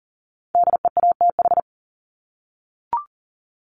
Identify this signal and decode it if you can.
Morse code, heard as a keyed tone, about 40 wpm, 700 Hz. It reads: DEATH
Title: Top 1001-1250 Words 40wpm